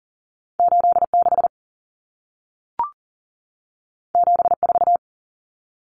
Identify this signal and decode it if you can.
Morse code, heard as a keyed tone, about 40 wpm, 700 Hz. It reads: 86 74
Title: Numbers - 2-Digit 40wpm